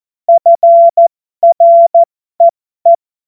Morse code, heard as a keyed tone, about 14 wpm, spaced as for 12 wpm, 680 Hz. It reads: FREE
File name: Morsekurs / Witze